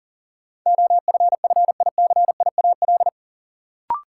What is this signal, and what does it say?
Morse code, heard as a keyed tone, about 40 wpm, 700 Hz. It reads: OFFICIAL
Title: Sentences from Top 1000 Words - New Word Drill - 40wpm